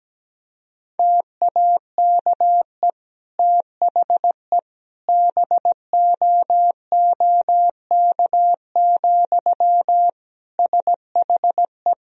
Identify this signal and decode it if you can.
Morse code, heard as a keyed tone, about 17 wpm, 700 Hz. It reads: TAKE THE BOOK, SHE
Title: Sentences from Top 1000 Words - Review Part 1 - 17wpm